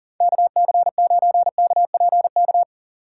Morse code, heard as a keyed tone, about 51 wpm, 700 Hz. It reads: KC9KPK